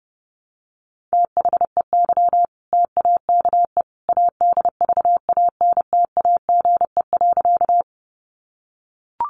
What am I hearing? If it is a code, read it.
Morse code, heard as a keyed tone, about 30 wpm, 700 Hz. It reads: THEY TAKE ADVANTAGE.
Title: Sentences from Top 900 Words 30wpm